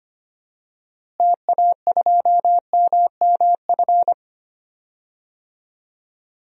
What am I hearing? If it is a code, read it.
Morse code, heard as a keyed tone, about 25 wpm, 700 Hz. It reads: TA2MMF
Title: DXCC-100 Call Signs - 6 Characters - Repeated 3x - 25wpm